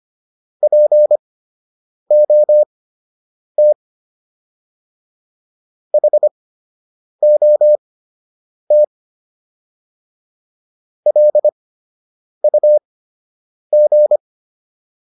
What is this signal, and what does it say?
Morse code, heard as a keyed tone, about 25 wpm, spaced as for 8 wpm, 600 Hz. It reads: POT HOT LUG